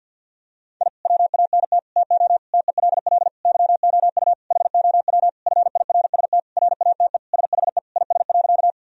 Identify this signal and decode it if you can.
Morse code, heard as a keyed tone, about 50 wpm, 700 Hz. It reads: I WANT TO TELL YOU HOW FIRST RATE SHE IS.
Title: Sentences from Top 1000 Words - Review Part 4 - 50wpm